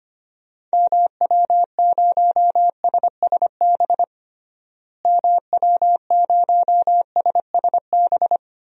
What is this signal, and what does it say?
Morse code, heard as a keyed tone, about 25 wpm, 700 Hz. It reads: MW0SSB MW0SSB
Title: DXCC-100 Call Signs - 6 Characters - Repeated 3x - 25wpm